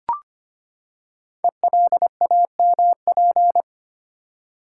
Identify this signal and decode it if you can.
Morse code, heard as a keyed tone, about 25 wpm, 700 Hz. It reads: ELAMP
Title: Code Groups - 5 Letters - 25wpm